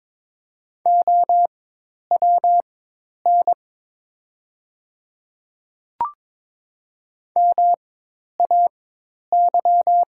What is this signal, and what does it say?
Morse code, heard as a keyed tone, about 22 wpm, spaced as for 10 wpm, 690 Hz. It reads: OWN MAY
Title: 4x Character Spacing - Top 300 Words - 22wpm